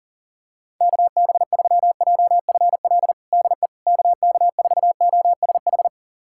Morse code, heard as a keyed tone, about 40 wpm, 700 Hz. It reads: KB3JFL DE KK4OSH
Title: USA Callsign DE Callsign 40wpm